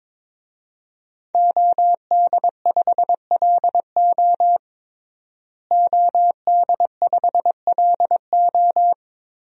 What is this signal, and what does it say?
Morse code, heard as a keyed tone, about 22 wpm, 700 Hz. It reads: OD5LO OD5LO
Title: DXCC-100 Call Signs - 5 Characters - Repeated 3x - 22wpm